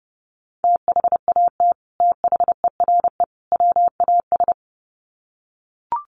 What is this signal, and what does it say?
Morse code, heard as a keyed tone, about 30 wpm, 700 Hz. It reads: THAT THERE WAS
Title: Sets of 3 Words 30wpm